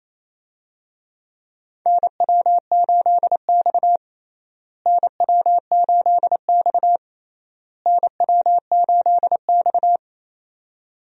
Morse code, heard as a keyed tone, about 28 wpm, 700 Hz. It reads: NW8X NW8X NW8X
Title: USA Call Signs - 4 Characters - Repeated 3x - 28wpm